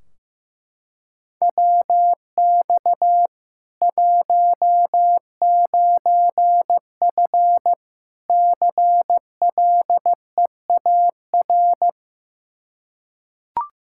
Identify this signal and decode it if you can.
Morse code, heard as a keyed tone, about 15 wpm, 700 Hz. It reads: WX 19F CLEAR
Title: Practice QSO - Exchange 2 - Info - 15wpm